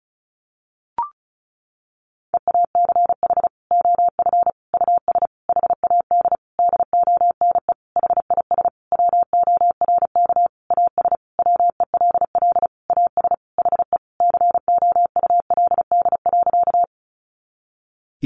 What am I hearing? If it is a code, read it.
Morse code, heard as a keyed tone, about 34 wpm, 700 Hz. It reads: EACH OF US HAD DONE HIS WORK AS WELL AS HE COULD.
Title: Sentences from Top 500 Words 35wpm